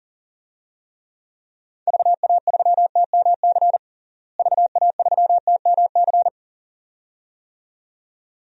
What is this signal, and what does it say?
Morse code, heard as a keyed tone, about 40 wpm, 700 Hz. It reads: VA3TMC VA3TMC
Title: DXCC-100 Call Signs - 6 Characters - Repeated 2x - 40wpm